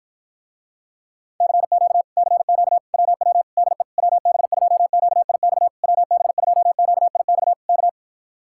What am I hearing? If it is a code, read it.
Morse code, heard as a keyed tone, about 52 wpm, 700 Hz. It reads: CQ CQ WW DE WB2QIX WB2QIX K